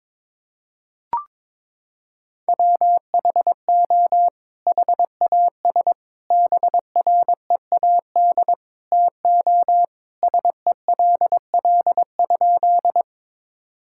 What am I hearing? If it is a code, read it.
Morse code, heard as a keyed tone, about 22 wpm, 700 Hz. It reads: WHO HAS BREAD TO SELL?
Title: Sentences from Top 1600 Words 22wpm